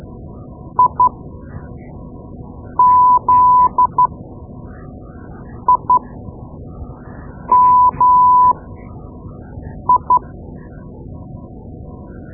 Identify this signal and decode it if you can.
Morse code, heard as a keyed tone, about 12 wpm, spaced as for 5 wpm, 1000 Hz. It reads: IZIMI
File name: Morse Lessons / Morse Lessons / Tape-2B